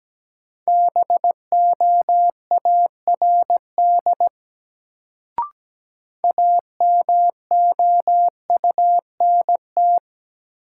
Morse code, heard as a keyed tone, about 17 wpm, 700 Hz. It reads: BOARD AMOUNT
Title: Sentences from Top 700 Words - New Word Drill - 17wpm